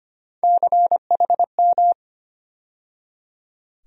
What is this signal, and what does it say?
Morse code, heard as a keyed tone, about 25 wpm, 700 Hz. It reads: CHM
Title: USA Call Sign Postfixes - 3 Letters 25wpm